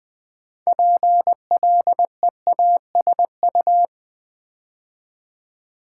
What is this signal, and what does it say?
Morse code, heard as a keyed tone, about 20 wpm, 700 Hz. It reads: PLEASU
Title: Build A Word - 3 Letters At A Time - 8-Letter Words - 20wpm